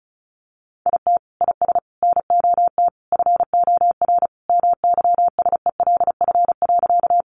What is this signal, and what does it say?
Morse code, heard as a keyed tone, about 34 wpm, 700 Hz. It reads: IT IS NOT FOR MYSELF.
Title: Sentences from Top 1300 Words 35wpm